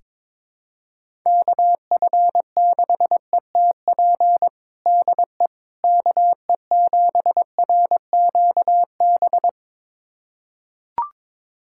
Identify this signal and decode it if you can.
Morse code, heard as a keyed tone, about 22 wpm, 700 Hz. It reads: KF6ETP DE KE7RQB
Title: USA Callsign DE Callsign - 6 Characters - 22wpm